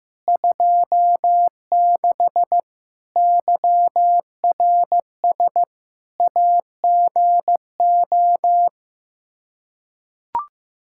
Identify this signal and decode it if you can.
Morse code, heard as a keyed tone, about 15 wpm, 700 Hz. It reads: 26 YRS AGO
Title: Practice QSO - Exchange 3 - More Info - 15wpm